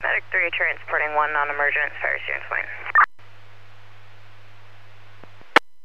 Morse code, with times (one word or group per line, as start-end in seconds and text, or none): none